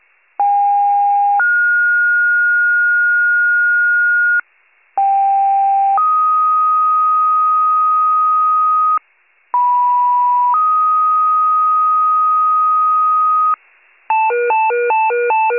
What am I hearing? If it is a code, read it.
Two-tone page, heard as a one-second tone, then a three-second tone, about 810 then 1430 Hz, about 790 then 1250 Hz, about 980 then 1250 Hz. It is heard from 0.4 to 13.5 s.